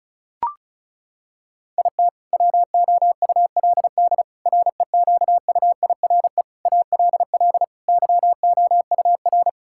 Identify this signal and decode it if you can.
Morse code, heard as a keyed tone, about 34 wpm, 700 Hz. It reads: IT WOULD REQUIRE ALL YOUR
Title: Sentences from Top 1000 Words - Review Part 6 - 35wpm